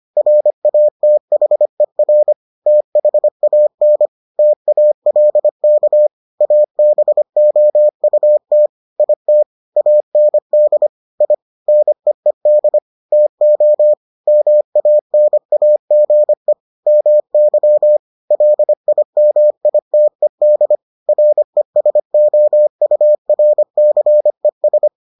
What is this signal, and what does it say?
Morse code, heard as a keyed tone, about 25 wpm, 600 Hz. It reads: RATHER THAN TALK ABOUT IT AND I NEED TO MANAGE MY LIMITED RESOURCES